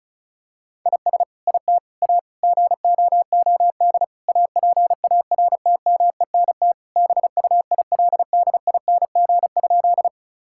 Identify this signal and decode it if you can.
Morse code, heard as a keyed tone, about 36 wpm, 700 Hz. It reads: IS IT A GOOD APARTMENT BUILDING?
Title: Sentences from Top 1000 Words - Review Part 6 - 35wpm